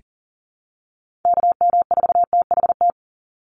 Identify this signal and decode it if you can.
Morse code, heard as a keyed tone, about 40 wpm, 700 Hz. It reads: KM4THT
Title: USA Call Signs 40wpm